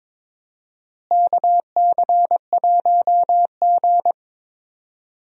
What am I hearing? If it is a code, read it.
Morse code, heard as a keyed tone, about 22 wpm, 700 Hz. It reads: KC1G
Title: USA Call Signs - 4 Characters - Repeated 3x - 22wpm